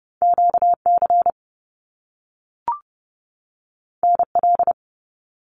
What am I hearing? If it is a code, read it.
Morse code, heard as a keyed tone, about 30 wpm, 700 Hz. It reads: QC NL
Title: Canadian Provinces and Territories 30wpm